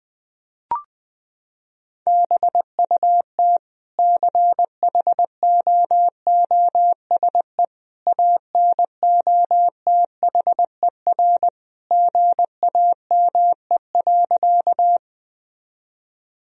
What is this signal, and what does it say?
Morse code, heard as a keyed tone, about 21 wpm, 700 Hz. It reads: BUT CHOOSE ANOTHER GAME.
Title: Sentences from Top 1000 Words - Review Part 4 - 20wpm